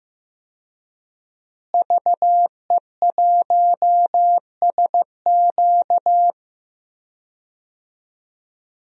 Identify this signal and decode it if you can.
Morse code, heard as a keyed tone, about 15 wpm, 700 Hz. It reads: VE1SQ
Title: DXCC-100 Call Signs - 5 Characters - Repeated 2x - 15wpm